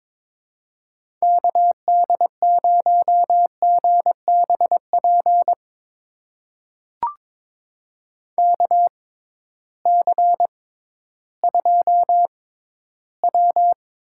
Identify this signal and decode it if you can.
Morse code, heard as a keyed tone, about 22 wpm, spaced as for 13 wpm, 700 Hz. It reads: KD0GBP KC2W
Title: USA Call Signs - 6 Characters - 6x Character Spacing - 22wpm